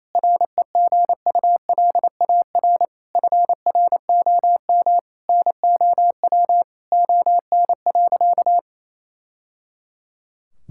Morse code, heard as a keyed tone, about 28 wpm, 700 Hz. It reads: REGULAR FROM NOW ON.
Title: Sentences from Top 1000 Words - Review Part 6 - 28wpm